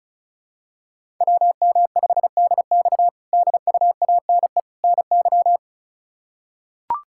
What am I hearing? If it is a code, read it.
Morse code, heard as a keyed tone, about 34 wpm, 700 Hz. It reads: WM5DX DUANE NY
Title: K1USN SST QSO Practice - 35wpm